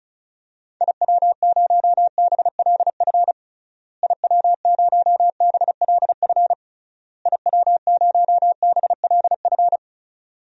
Morse code, heard as a keyed tone, about 36 wpm, 700 Hz. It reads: IW0BLF IW0BLF IW0BLF
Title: DXCC-100 Call Signs - 6 Characters - Repeated 3x - 35wpm